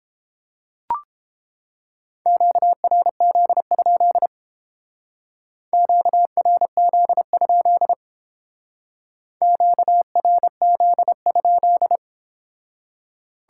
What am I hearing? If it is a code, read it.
Morse code, heard as a keyed tone, about 31 wpm, 700 Hz. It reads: QRZ? QRZ? QRZ?